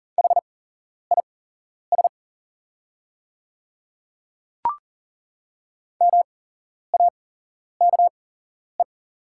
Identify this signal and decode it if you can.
Morse code, heard as a keyed tone, about 38 wpm, spaced as for 11 wpm, 700 Hz. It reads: HIS MAKE
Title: 8x Character Spacing - Top 200 Words - 40wpm